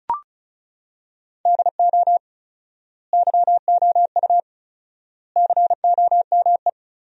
Morse code, heard as a keyed tone, about 36 wpm, 700 Hz. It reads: DO YOU COME
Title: 4x Word-Spacing - Sentences from Top 200 Words - 35wpm